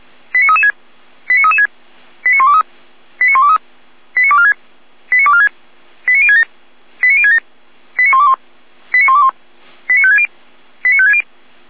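Five-tone selective call (ZVEI), heard as quick runs of five tones, tones of about 70 ms. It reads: 89336 89336 89123 89123 89246 89246 89067 89067 89121 89121 89560 89560
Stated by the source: ZVEI five-tone selective call received on an SDR.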